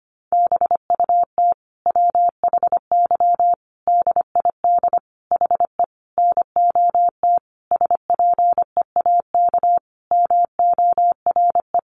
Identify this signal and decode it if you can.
Morse code, heard as a keyed tone, about 25 wpm, 700 Hz. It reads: BUT WHY DID HE NOT SPEAK MORE
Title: Sentences from Top 1300 Words 25wpm